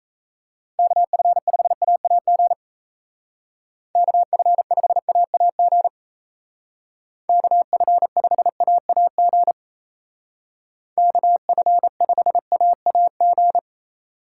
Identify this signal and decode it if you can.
Morse code, heard as a keyed tone, about 34 wpm, 700 Hz. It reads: KF5AAG KF5AAG KF5AAG KF5AAG
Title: Speed Racing - USA 2x3 Call Signs 28wpm